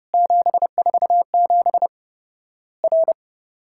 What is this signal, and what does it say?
Morse code, heard as a keyed tone, about 30 wpm, 700 Hz. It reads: 747 R
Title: Worked All Europe DX Contest - QTC Exchange - 30wpm